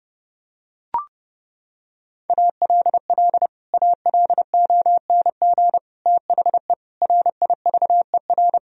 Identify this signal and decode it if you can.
Morse code, heard as a keyed tone, about 30 wpm, 700 Hz. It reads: ALL ALONG THE RIVER
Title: Sentences from Top 1000 Words - Review Part 2 - 30wpm